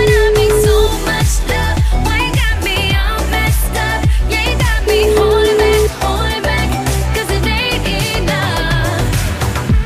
Ringback tone (Europe, 425 Hz), with noise.